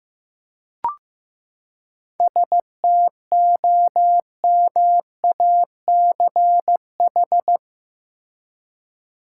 Morse code, heard as a keyed tone, about 15 wpm, 700 Hz. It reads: STOMACH
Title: Seven Letter Words - 15wpm